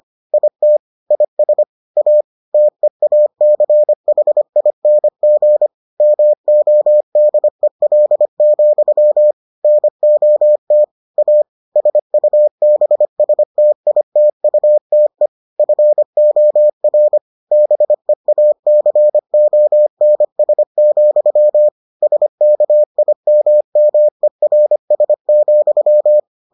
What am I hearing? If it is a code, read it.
Morse code, heard as a keyed tone, about 25 wpm, 600 Hz. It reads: IT IS A TEACHING MODEL, NOT A SUBSTITUTE FOR BEACONS, SKIMMERS,